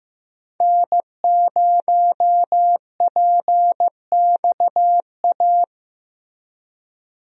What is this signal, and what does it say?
Morse code, heard as a keyed tone, about 15 wpm, 700 Hz. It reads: N0PXA
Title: USA Call Signs - 5 Characters - Repeated 3x - 15wpm